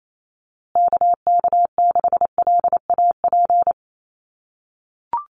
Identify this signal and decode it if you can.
Morse code, heard as a keyed tone, about 28 wpm, 700 Hz. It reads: KK6LAP
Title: USA Call Signs Repeated 2x 28wpm